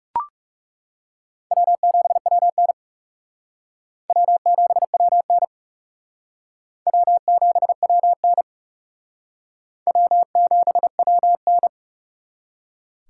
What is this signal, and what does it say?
Morse code, heard as a keyed tone, about 38 wpm, 700 Hz. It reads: W7WN W7WN W7WN W7WN